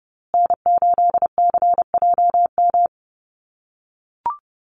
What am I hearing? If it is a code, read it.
Morse code, heard as a keyed tone, about 30 wpm, 700 Hz. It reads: N8CJM
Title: USA Call Signs Repeated 3x 30wpm